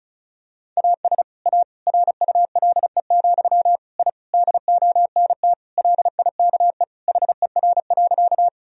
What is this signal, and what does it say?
Morse code, heard as a keyed tone, about 34 wpm, 700 Hz. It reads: AS A RULE, I DONT LIKE HER.
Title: Sentences from Top 1000 Words - Review Part 2 - 35wpm